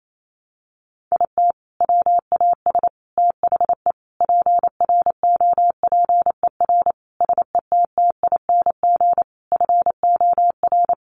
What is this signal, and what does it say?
Morse code, heard as a keyed tone, about 28 wpm, 700 Hz. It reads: IT WAS THE PROPER SETTING FOR
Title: Sentences from Top 1300 Words 28wpm